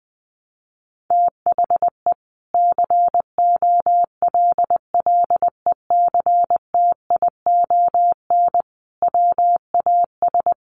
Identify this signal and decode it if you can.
Morse code, heard as a keyed tone, about 20 wpm, 700 Hz. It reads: THE COLLECTION WAS
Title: Sentences from Top 1200 Words 20wpm